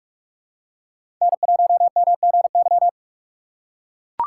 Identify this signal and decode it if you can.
Morse code, heard as a keyed tone, about 43 wpm, 700 Hz. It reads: N1MGY
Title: USA Call Signs - 5 Characters - 6x Character Spacing - 45wpm